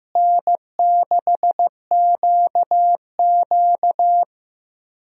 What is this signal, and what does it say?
Morse code, heard as a keyed tone, about 15 wpm, 700 Hz. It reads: N6QQ
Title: USA Call Signs - 4 Characters - Repeated 2x - 15wpm